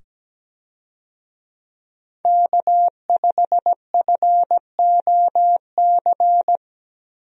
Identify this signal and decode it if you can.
Morse code, heard as a keyed tone, about 17 wpm, 700 Hz. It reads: K5FOC
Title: USA Call Signs - 5 Characters - 17wpm